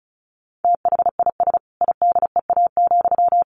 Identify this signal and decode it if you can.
Morse code, heard as a keyed tone, about 34 wpm, 700 Hz. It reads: THIS IDEA,
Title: Sentences from Top 1300 Words 35wpm